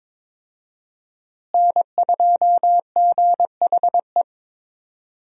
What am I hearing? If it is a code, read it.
Morse code, heard as a keyed tone, about 22 wpm, 700 Hz. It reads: N2GHE
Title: USA Call Signs - 5 Characters - Repeated 3x - 22wpm